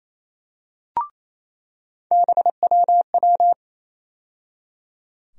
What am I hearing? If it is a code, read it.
Morse code, heard as a keyed tone, about 28 wpm, 700 Hz. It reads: BWW
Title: USA Call Sign Postfixes - 3 Letters 28wpm